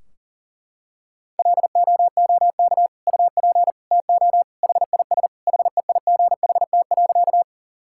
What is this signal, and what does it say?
Morse code, heard as a keyed tone, about 37 wpm, 700 Hz. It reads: LOOK UP TO HIS HEIGHT.
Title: Sentences from Top 1600 Words 40wpm